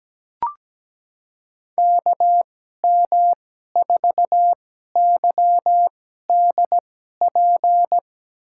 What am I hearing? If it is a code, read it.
Morse code, heard as a keyed tone, about 17 wpm, spaced as for 12 wpm, 700 Hz. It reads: KM4YDP
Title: USA Call Signs - 2x Character Spacing - 17wpm